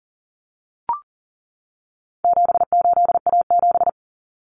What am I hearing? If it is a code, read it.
Morse code, heard as a keyed tone, about 40 wpm, 700 Hz. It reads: 78A7
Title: Common Cut Numbers - 4-Digit 40wpm